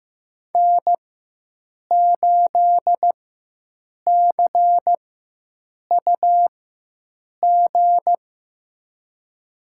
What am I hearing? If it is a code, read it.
Morse code, heard as a keyed tone, about 15 wpm, spaced as for 7 wpm, 700 Hz. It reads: N8CUG
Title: USA Call Signs - 4x Character Spacing - 15wpm